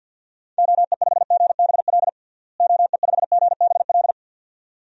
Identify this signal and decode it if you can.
Morse code, heard as a keyed tone, about 52 wpm, 700 Hz. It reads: OE5GBL OE5GBL
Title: DXCC-100 Call Signs - 6 Characters - Repeated 2x - 50wpm